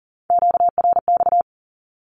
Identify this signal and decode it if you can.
Morse code, heard as a keyed tone, about 40 wpm, 700 Hz. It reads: QRX
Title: Q-Codes Repeated 3x 40wpm